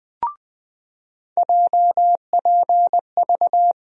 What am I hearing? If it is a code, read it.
Morse code, heard as a keyed tone, about 20 wpm, 700 Hz. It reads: JPV
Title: USA Call Sign Postfixes - 3 Letters 20wpm